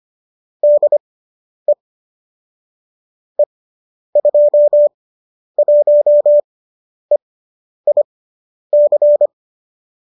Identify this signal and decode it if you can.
Morse code, heard as a keyed tone, about 25 wpm, spaced as for 10 wpm, 590 Hz. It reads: DE E21EIC
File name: qso005 10